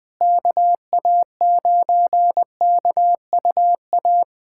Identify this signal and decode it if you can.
Morse code, heard as a keyed tone, about 20 wpm, 700 Hz. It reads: KA9KUA
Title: USA Callsign DE Callsign 20wpm